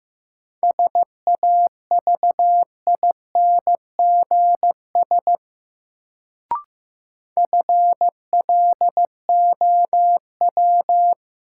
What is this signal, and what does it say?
Morse code, heard as a keyed tone, about 15 wpm, 700 Hz. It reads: SAVINGS FLOW
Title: Sentences from Top 1200 Words - New Word Drill - 15wpm